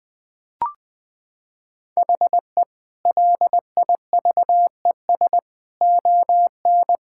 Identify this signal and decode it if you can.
Morse code, heard as a keyed tone, about 20 wpm, 700 Hz. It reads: HE LIVES ON
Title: Sentences with 4-Word Phrases - 20wpm